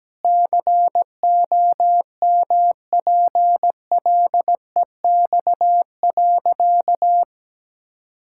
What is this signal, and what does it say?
Morse code, heard as a keyed tone, about 17 wpm, 700 Hz. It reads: COMPLEX.
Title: Sentences from Top 1000 Words - Review Part 6 - 17wpm